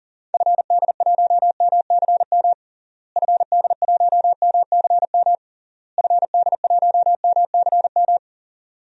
Morse code, heard as a keyed tone, about 40 wpm, 700 Hz. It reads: FD1MCM FD1MCM FD1MCM
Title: DXCC-100 Call Signs - 6 Characters - Repeated 3x - 40wpm